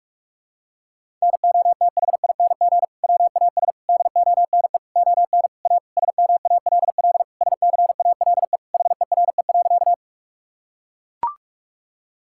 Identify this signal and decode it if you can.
Morse code, heard as a keyed tone, about 41 wpm, 700 Hz. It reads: NOTHING WAS DONE ON A SMALL SCALE HERE.
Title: Sentences from Top 1000 Words - Review Part 5 - 45wpm